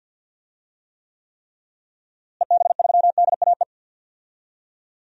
Morse code, heard as a keyed tone, about 47 wpm, 700 Hz. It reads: EB3DAE
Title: DXCC-100 Call Signs - 6 Characters - Repeated 3x - 50wpm